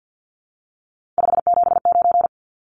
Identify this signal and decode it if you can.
Morse code, heard as a keyed tone, about 49 wpm, 700 Hz. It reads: 579
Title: Numbers - 3-Digit 50wpm